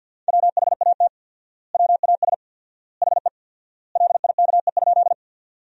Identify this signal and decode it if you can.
Morse code, heard as a keyed tone, about 50 wpm, 700 Hz. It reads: WHAT WAS HE LIKE?
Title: 4x Word-Spacing - Sentences from Top 100 Words - 50wpm